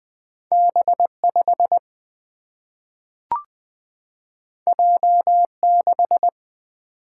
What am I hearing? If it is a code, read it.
Morse code, heard as a keyed tone, about 20 wpm, 700 Hz. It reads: B5 J6
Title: Single-Letter Single-Number 20wpm